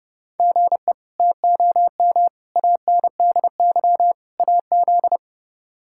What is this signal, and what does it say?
Morse code, heard as a keyed tone, about 32 wpm, 700 Hz. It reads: GE TOM ANDY AZ
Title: K1USN SST QSO Practice - 30wpm